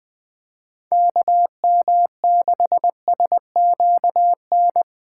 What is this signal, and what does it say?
Morse code, heard as a keyed tone, about 21 wpm, 700 Hz. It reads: KM6SQN
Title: USA Call Signs - 4x Character Spacing - 20wpm